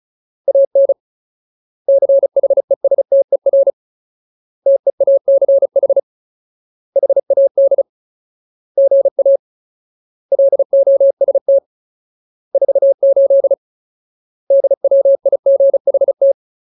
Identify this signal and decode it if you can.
Morse code, heard as a keyed tone, about 34 wpm, 550 Hz. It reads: AN CHESTER TEACH HAD GA LOST 48 DWIGHT
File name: Ultimate Word List / 35WPM Ultimate